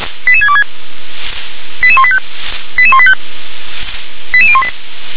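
Five-tone selective call (ZVEI), heard as quick runs of five tones, tones of about 70 ms. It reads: 88626 88185 88185 88018
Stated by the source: ZVEI five-tone selective call received on SDR.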